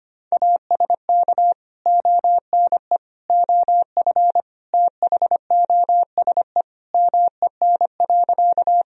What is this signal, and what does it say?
Morse code, heard as a keyed tone, about 25 wpm, 700 Hz. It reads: ASK ONE OF THOSE MEN.
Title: Sentences from Top 1000 Words - Review Part 1 - 25wpm